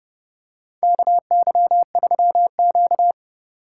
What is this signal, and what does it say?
Morse code, heard as a keyed tone, about 30 wpm, 700 Hz. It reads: KY3Q
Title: USA Call Signs - 2x Character Spacing - 30wpm